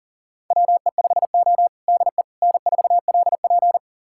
Morse code, heard as a keyed tone, about 40 wpm, 700 Hz. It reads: WE5O DE N4LP
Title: USA Callsign DE Callsign - 4 Characters - 40wpm